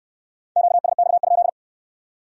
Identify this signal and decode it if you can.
Morse code, heard as a keyed tone, about 62 wpm, 700 Hz. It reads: QSX?